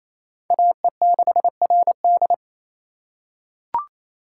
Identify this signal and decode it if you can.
Morse code, heard as a keyed tone, about 27 wpm, 700 Hz. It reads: AE6RD
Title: USA Call Signs - 5 Characters - 4x Character Spacing - 28wpm